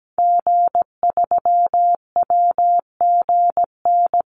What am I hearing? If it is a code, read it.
Morse code, heard as a keyed tone, about 17 wpm, 700 Hz. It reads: G3WGN
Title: Call Signs - 17wpm